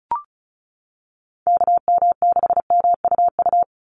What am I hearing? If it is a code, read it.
Morse code, heard as a keyed tone, about 34 wpm, 700 Hz. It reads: KM6MUU
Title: USA Call Signs 35wpm